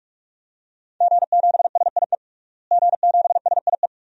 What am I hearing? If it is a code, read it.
Morse code, heard as a keyed tone, about 43 wpm, 700 Hz. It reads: G7SIE G7SIE
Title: DXCC-100 Call Signs - 5 Characters - Repeated 2x - 45wpm